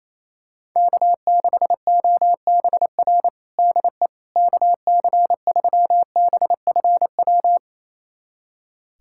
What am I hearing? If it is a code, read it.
Morse code, heard as a keyed tone, about 28 wpm, 700 Hz. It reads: K6OBR DE KC3BFW
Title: USA Callsign DE Callsign 28wpm